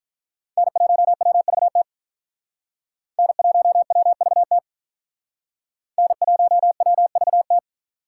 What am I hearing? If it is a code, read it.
Morse code, heard as a keyed tone, about 44 wpm, 700 Hz. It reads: N1WVT N1WVT N1WVT
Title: Speed Racing - USA 1x3 Call Signs 35wpm